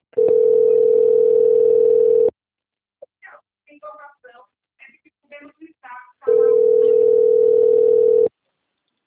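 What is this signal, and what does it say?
Ringback tone (North America), 440 + 480 Hz.
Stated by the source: Outgoing telephone call recorded on a system labelled Rio 3 - Ura.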